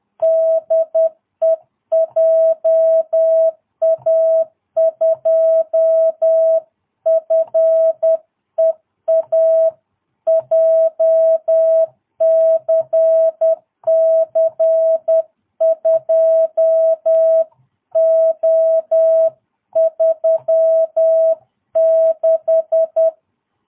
Morse code, heard as a keyed tone, about 10 wpm, 660 Hz. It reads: DE JA2FEA JCC 2O36
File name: jh8cbh / 2022cw